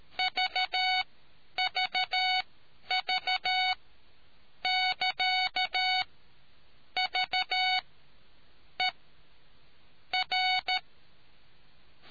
Morse code, heard as a keyed tone, about 13 wpm, spaced as for 7 wpm, 740 Hz. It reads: VVV<KA>VER